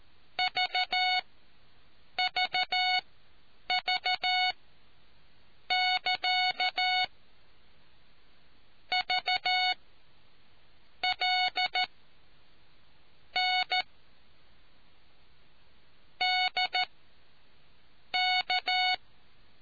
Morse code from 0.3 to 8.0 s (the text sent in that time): VVV<KA>